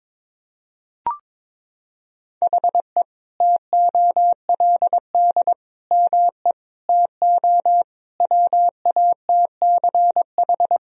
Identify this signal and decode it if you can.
Morse code, heard as a keyed tone, about 22 wpm, 700 Hz. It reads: HE TOLD ME TO WATCH
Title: Sentences from Top 1000 Words - Review Part 1 - 22wpm